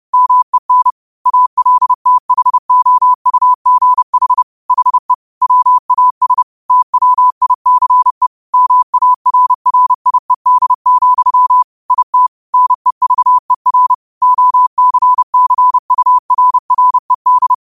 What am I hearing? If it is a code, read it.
Morse code, heard as a keyed tone, about 30 wpm, 1000 Hz. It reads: MEN ALTHOUGH HE WAS TWICE MARRIED, IT NEVER OCCURRED